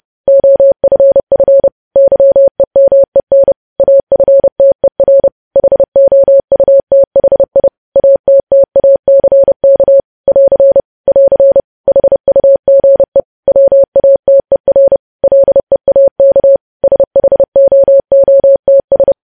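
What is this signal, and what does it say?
Morse code, heard as a keyed tone, about 30 wpm, 550 Hz. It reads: OFF YEMEN AFTER HOUTHI ATTACK <AR> <AR> HUGE WATER LEAK SHOOTS
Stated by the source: News headlines sent in Morse at 30.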